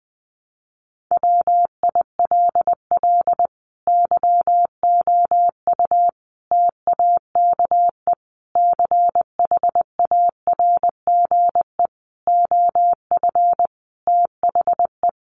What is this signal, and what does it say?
Morse code, heard as a keyed tone, about 20 wpm, 700 Hz. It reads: WILL YOU TAKE CHARGE OF THE
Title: Sentences from Top 900 Words 20wpm